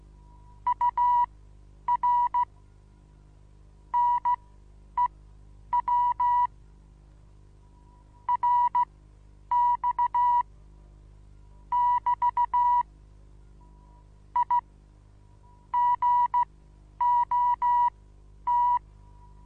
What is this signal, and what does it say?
Morse code, heard as a keyed tone, about 15 wpm, spaced as for 9 wpm, 970 Hz. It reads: UR NEW RX <BT> I GOT
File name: Onlinelehrgang / morsen / DJ4UF Morsen25